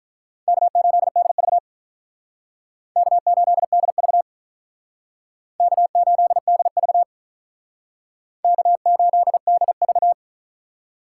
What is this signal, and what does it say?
Morse code, heard as a keyed tone, about 48 wpm, 700 Hz. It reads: K8DV K8DV K8DV K8DV